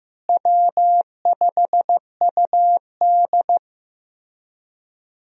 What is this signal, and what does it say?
Morse code, heard as a keyed tone, about 15 wpm, 700 Hz. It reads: W5UD
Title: USA Call Signs - 4 Characters - Repeated 3x - 15wpm